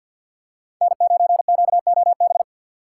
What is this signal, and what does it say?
Morse code, heard as a keyed tone, about 48 wpm, 700 Hz. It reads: N9QOB
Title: USA Call Signs - 5 Characters - 6x Character Spacing - 50wpm